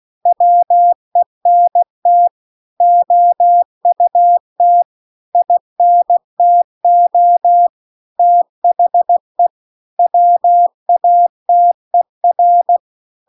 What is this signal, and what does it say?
Morse code, heard as a keyed tone, about 16 wpm, 700 Hz. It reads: WENT OUT INTO THE WATER